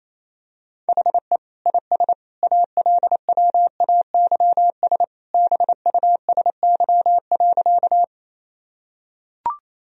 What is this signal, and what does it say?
Morse code, heard as a keyed tone, about 28 wpm, 700 Hz. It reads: HE IS ALWAYS BUSY.
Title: Sentences from Top 1000 Words - Review Part 3 - 28wpm